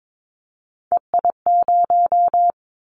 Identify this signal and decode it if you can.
Morse code, heard as a keyed tone, about 22 wpm, 700 Hz. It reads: EI0
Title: Callsign Prefixes for Top 50 DXCC - 22wpm